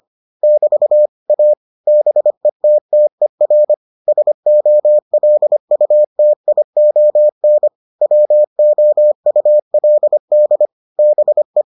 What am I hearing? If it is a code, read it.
Morse code, heard as a keyed tone, about 25 wpm, 600 Hz. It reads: <BT> A BETTER SOLUTION WOULD BE